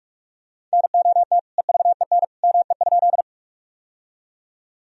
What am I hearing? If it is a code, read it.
Morse code, heard as a keyed tone, about 42 wpm, 700 Hz. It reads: NOT EVEN ME?
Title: Sentences from Top 1000 Words - Review Part 1 - 45wpm